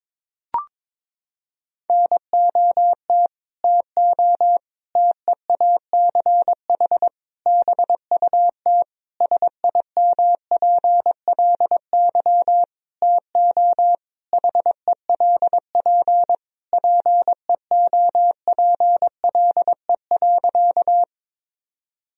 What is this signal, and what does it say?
Morse code, heard as a keyed tone, about 22 wpm, 700 Hz. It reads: NOT TO TEACH BUT SIMPLY TO HELP PEOPLE.
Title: Sentences from Top 1000 Words - Review Part 4 - 22wpm